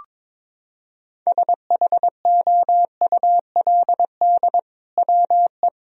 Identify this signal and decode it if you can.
Morse code, heard as a keyed tone, about 22 wpm, 700 Hz. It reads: SHOULD WE
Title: Sentences from Top 1000 Words - Review Part 3 - 22wpm